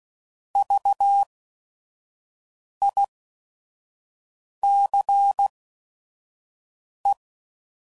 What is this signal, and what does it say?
Morse code, heard as a keyed tone, about 16 wpm, spaced as for 5 wpm, 800 Hz. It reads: VICE
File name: Prefixes / By First Letter / Vs